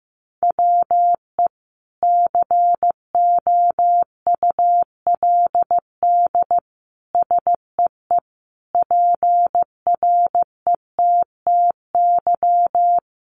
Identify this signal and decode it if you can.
Morse code, heard as a keyed tone, about 15 wpm, 700 Hz. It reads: WE COULD SEE PRETTY
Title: Sentences from Top 1000 Words 15wpm